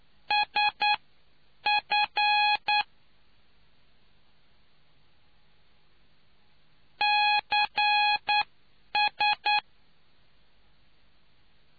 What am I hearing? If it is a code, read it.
Morse code, heard as a keyed tone, about 9 wpm, spaced as for 8 wpm, 830 Hz. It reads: SF CS